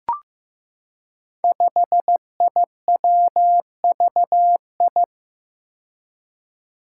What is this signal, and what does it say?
Morse code, heard as a keyed tone, about 15 wpm, 700 Hz. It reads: 5IWVI
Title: Five Challenging Letter-Number Combinations 15wpm